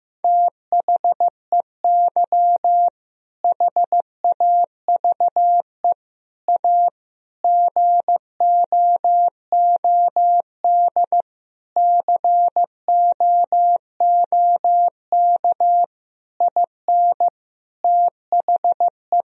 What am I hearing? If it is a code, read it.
Morse code, heard as a keyed tone, about 15 wpm, 700 Hz. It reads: THEY HAVE A GOOD COOK IN THE
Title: Sentences from Top 1000 Words - Review Part 4 - 15wpm